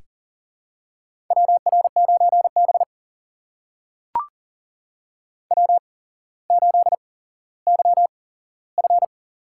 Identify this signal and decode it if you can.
Morse code, heard as a keyed tone, about 38 wpm, spaced as for 17 wpm, 700 Hz. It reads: WR9B W8YF
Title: USA Call Signs - 4 Characters - 8x Character Spacing - 40wpm